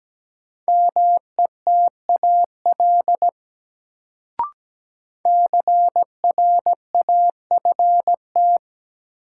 Morse code, heard as a keyed tone, about 17 wpm, 700 Hz. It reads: METAL CRAFT
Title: Sentences from Top 1000 Words - Review Part 6 - New Word Drill - 17wpm